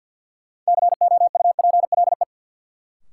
Morse code, heard as a keyed tone, about 50 wpm, 700 Hz. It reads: COUPLE
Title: Sets of 2 Words - New Word Drill - 50wpm